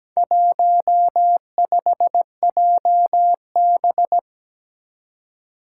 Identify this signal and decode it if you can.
Morse code, heard as a keyed tone, about 17 wpm, 700 Hz. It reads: 15JB
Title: Four Challenging Letter-Number Combinations 17wpm